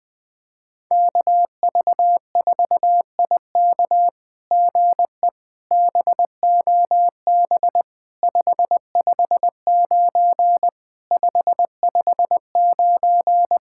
Morse code, heard as a keyed tone, about 20 wpm, 700 Hz. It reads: KV4IK GE BOB 559 559
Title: SOTA QSO Practice 20wpm